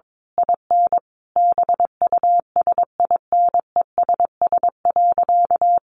Morse code, heard as a keyed tone, about 22 wpm, 700 Hz. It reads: IN BUSINESS.
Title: Sentences from Top 1300 Words 22wpm